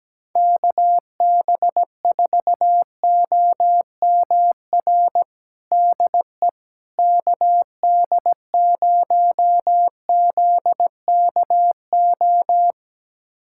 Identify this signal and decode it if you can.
Morse code, heard as a keyed tone, about 17 wpm, 700 Hz. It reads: KB4OMR DE KD0ZKO
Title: USA Callsign DE Callsign - 6 Characters - 17wpm